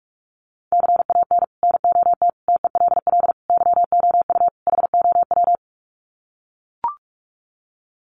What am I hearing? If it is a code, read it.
Morse code, heard as a keyed tone, about 44 wpm, 700 Hz. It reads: CAN NOT TELL YOU HOW
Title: Sets of 5 Words Encore2 45wpm